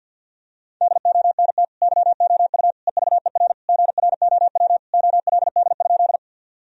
Morse code, heard as a keyed tone, about 49 wpm, 700 Hz. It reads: DONT YOU EVER GROW OLD?
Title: Sentences from Top 1000 Words - Review Part 1 - 50wpm